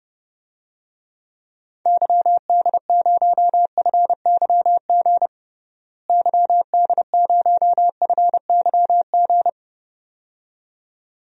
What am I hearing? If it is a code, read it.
Morse code, heard as a keyed tone, about 32 wpm, 700 Hz. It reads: YD0FYG YD0FYG
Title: DXCC-100 Call Signs - 6 Characters - Repeated 2x - 30wpm